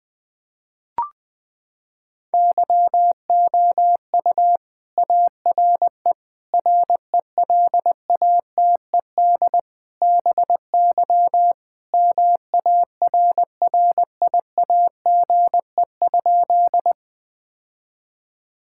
Morse code, heard as a keyed tone, about 20 wpm, 700 Hz. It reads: YOU ARE RELATED BY MARRIAGE?
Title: Sentences from Top 1000 Words - Review Part 5 - 20wpm